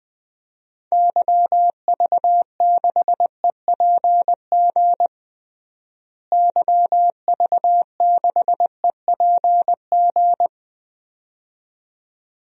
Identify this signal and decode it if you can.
Morse code, heard as a keyed tone, about 21 wpm, 700 Hz. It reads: YV6EPG YV6EPG
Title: DXCC-100 Call Signs - 6 Characters - Repeated 2x - 20wpm